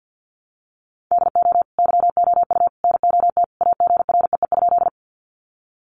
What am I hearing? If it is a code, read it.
Morse code, heard as a keyed tone, about 50 wpm, 700 Hz. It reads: DO YOU NOT AGREE?
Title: Sentences from Top 1100 Words 50wpm